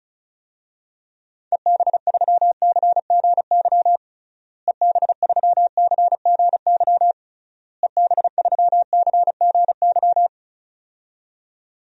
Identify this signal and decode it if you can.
Morse code, heard as a keyed tone, about 34 wpm, 700 Hz. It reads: EB3CGY EB3CGY EB3CGY
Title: DXCC-100 Call Signs - 6 Characters - Repeated 3x - 35wpm